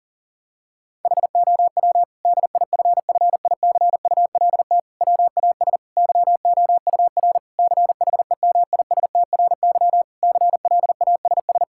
Morse code, heard as a keyed tone, about 40 wpm, 700 Hz. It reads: HOW DIFFICULT WAS YOUR CHEMISTRY CLASS
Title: Sentences from Top 1000 Words - Review Part 6 - 40wpm